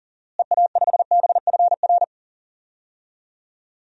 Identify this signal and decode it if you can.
Morse code, heard as a keyed tone, about 40 wpm, 700 Hz. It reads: EA5BFR
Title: DXCC-100 Call Signs - 6 Characters - Repeated 3x - 40wpm